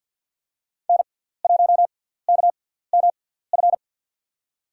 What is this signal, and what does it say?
Morse code, heard as a keyed tone, about 52 wpm, spaced as for 17 wpm, 700 Hz. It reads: N1KMF
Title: USA Call Signs - 5 Characters - 6x Character Spacing - 50wpm